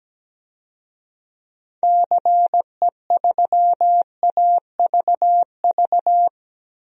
Morse code, heard as a keyed tone, about 17 wpm, 700 Hz. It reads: CE3AVV
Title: DXCC-100 Call Signs - 6 Characters - 17wpm